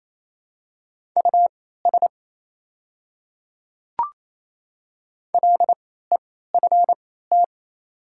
Morse code, heard as a keyed tone, about 28 wpm, spaced as for 16 wpm, 690 Hz. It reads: US LEFT
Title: 3x Character Spacing - Top 500 Words - 28wpm